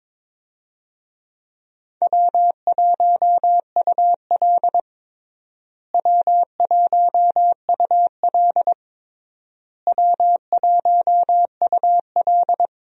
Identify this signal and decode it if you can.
Morse code, heard as a keyed tone, about 22 wpm, 700 Hz. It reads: W1UL W1UL W1UL
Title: USA Call Signs - 4 Characters - Repeated 3x - 22wpm